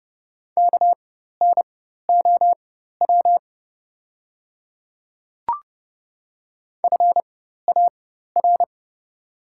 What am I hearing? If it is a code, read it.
Morse code, heard as a keyed tone, about 30 wpm, spaced as for 14 wpm, 700 Hz. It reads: KNOW FAR
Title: 4x Character Spacing - Top 300 Words - 30wpm